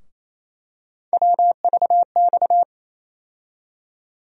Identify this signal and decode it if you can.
Morse code, heard as a keyed tone, about 28 wpm, 700 Hz. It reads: WVX
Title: USA Call Sign Postfixes - 3 Letters 28wpm